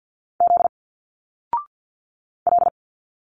Morse code, heard as a keyed tone, about 50 wpm, 700 Hz. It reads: Z L
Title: Single Letters 50wpm